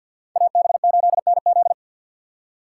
Morse code, heard as a keyed tone, about 53 wpm, 700 Hz. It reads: AB8NZ
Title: USA Call Signs - 5 Characters - 2x Character Spacing - 50wpm